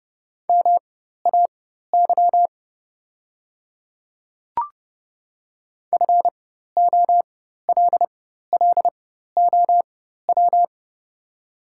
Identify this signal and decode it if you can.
Morse code, heard as a keyed tone, about 29 wpm, spaced as for 14 wpm, 700 Hz. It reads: MAY FOLLOW
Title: 4x Character Spacing - Top 300 Words - 30wpm